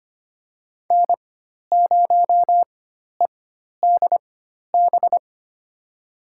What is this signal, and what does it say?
Morse code, heard as a keyed tone, about 25 wpm, spaced as for 12 wpm, 700 Hz. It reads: N0EDB
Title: USA Call Signs - 5 Characters - 4x Character Spacing - 25wpm